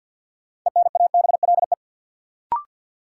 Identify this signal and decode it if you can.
Morse code, heard as a keyed tone, about 51 wpm, 700 Hz. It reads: ENABLE
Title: Sentences from Top 1500 Words - New Word Drill - 50wpm